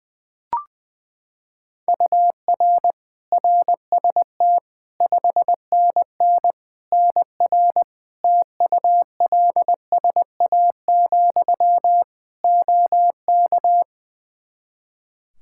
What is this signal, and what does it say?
Morse code, heard as a keyed tone, about 20 wpm, 700 Hz. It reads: UR RST 5NN NR TULSA, OK
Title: Practice QSO - Exchange 1 - Beginning - 20wpm